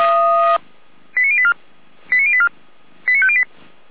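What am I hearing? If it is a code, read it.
Five-tone selective call (ZVEI), heard as quick runs of five tones, tones of about 70 ms. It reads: 89084 89084 89508
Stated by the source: ZVEI five-tone selective call 89057 received on SDR.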